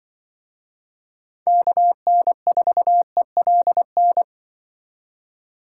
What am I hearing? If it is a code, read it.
Morse code, heard as a keyed tone, about 24 wpm, 700 Hz. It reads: KN4ELN
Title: Speed Racing - USA 2x3 Call Signs 20wpm